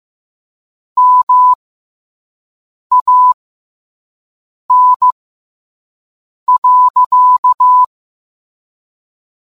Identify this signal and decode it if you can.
Morse code, heard as a keyed tone, about 15 wpm, spaced as for 5 wpm, 1000 Hz. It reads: MAN.